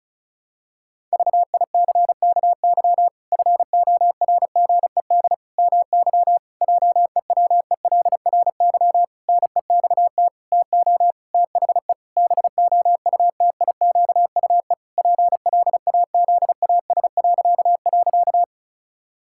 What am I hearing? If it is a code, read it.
Morse code, heard as a keyed tone, about 34 wpm, 700 Hz. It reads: VICKY FORGED MY JEWELRY NEXT TO THE BOUTIQUE PLAZAS..
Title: Pangram Challenge v2 - 35wpm